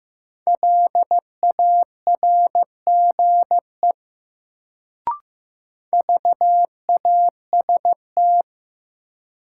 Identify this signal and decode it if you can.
Morse code, heard as a keyed tone, about 15 wpm, 700 Hz. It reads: LARGE VAST
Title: Sets of 4 Words - New Word Drill - 15wpm